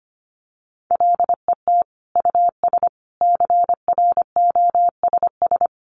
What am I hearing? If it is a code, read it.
Morse code, heard as a keyed tone, about 25 wpm, 700 Hz. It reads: LET US CROSS
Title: Sentences from Top 300 Words 25wpm